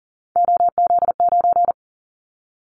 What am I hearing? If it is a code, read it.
Morse code, heard as a keyed tone, about 40 wpm, 700 Hz. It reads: OZ9
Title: Callsign Prefixes for Top 100 DXCC - 40wpm